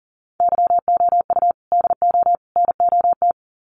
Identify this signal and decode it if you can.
Morse code, heard as a keyed tone, about 40 wpm, 700 Hz. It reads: YOU DO NOT
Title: Sets of 3 Words 40wpm